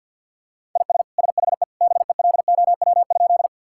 Morse code, heard as a keyed tone, about 50 wpm, 700 Hz. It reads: IS SHE BELOW?
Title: Sentences from Top 1000 Words - Review Part 5 - 50wpm